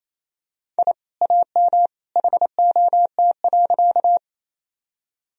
Morse code, heard as a keyed tone, about 28 wpm, 700 Hz. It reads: I AM HOT.
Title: Sentences from Top 1000 Words - Review Part 3 - 28wpm